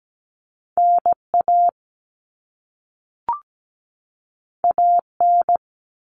Morse code, heard as a keyed tone, about 17 wpm, 700 Hz. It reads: NA AN
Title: All Cut Numbers - 2-Digit 17wpm